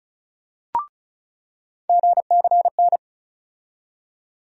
Morse code, heard as a keyed tone, about 34 wpm, 700 Hz. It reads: GCN
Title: USA Call Sign Postfixes - 3 Letters 35wpm